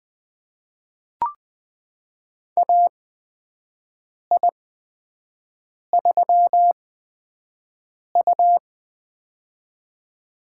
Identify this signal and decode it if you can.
Morse code, heard as a keyed tone, about 20 wpm, spaced as for 5 wpm, 700 Hz. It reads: AI3U
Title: USA Call Signs - 8x Character Spacing - 20wpm